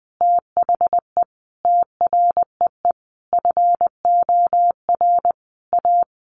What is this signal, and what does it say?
Morse code, heard as a keyed tone, about 20 wpm, 700 Hz. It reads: THE TREE FOR A
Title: Easy Phrases 20wpm